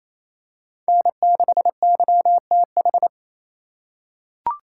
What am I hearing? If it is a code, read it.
Morse code, heard as a keyed tone, about 28 wpm, 700 Hz. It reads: N6YTH
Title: USA Call Signs - 5 Characters - 2x Character Spacing - 28wpm